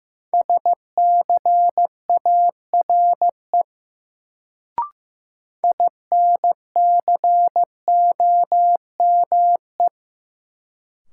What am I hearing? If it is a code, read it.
Morse code, heard as a keyed tone, about 15 wpm, 700 Hz. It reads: SCARE INCOME
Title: Sentences from Top 1000 Words - Review Part 5 - New Word Drill - 15wpm